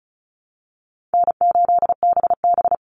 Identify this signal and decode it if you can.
Morse code, heard as a keyed tone, about 34 wpm, 700 Hz. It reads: N8BB
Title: NA SOTA Chaser Callsigns 35wpm